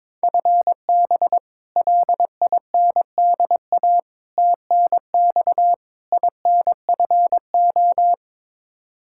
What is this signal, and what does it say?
Morse code, heard as a keyed tone, about 22 wpm, 700 Hz. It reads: FB LINDA TNX INFO
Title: Practice QSO - Exchange 3 - More Info - 22wpm